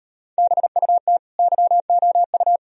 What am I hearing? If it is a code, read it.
Morse code, heard as a keyed tone, about 35 wpm, 700 Hz. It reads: BUT YOU